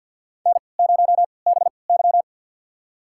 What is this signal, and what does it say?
Morse code, heard as a keyed tone, about 53 wpm, spaced as for 29 wpm, 700 Hz. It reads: N0BY
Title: USA Call Signs - 4 Characters - 3x Character Spacing - 50wpm